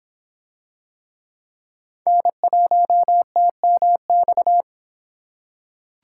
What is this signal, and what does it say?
Morse code, heard as a keyed tone, about 26 wpm, 700 Hz. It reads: N1TMX